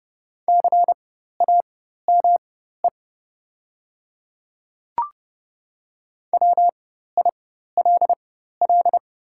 Morse code, heard as a keyed tone, about 30 wpm, spaced as for 14 wpm, 700 Hz. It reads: CAME WILL
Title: 4x Character Spacing - Top 200 Words - 30wpm